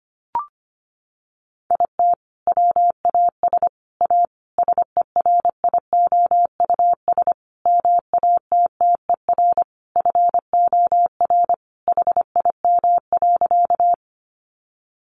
Morse code, heard as a keyed tone, about 25 wpm, 700 Hz. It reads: IT WAS A SERIOUS MATTER FOR HIM.
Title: Sentences from Top 1100 Words 25wpm